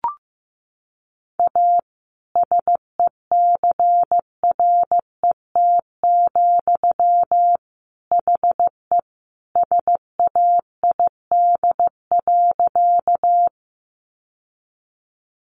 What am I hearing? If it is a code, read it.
Morse code, heard as a keyed tone, about 15 wpm, 700 Hz. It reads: A SECRET, HE SAID.
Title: Sentences from Top 1200 Words 15wpm